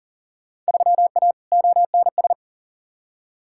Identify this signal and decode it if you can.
Morse code, heard as a keyed tone, about 40 wpm, 700 Hz. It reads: 3A ONS
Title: Field Day Exchanges - 40wpm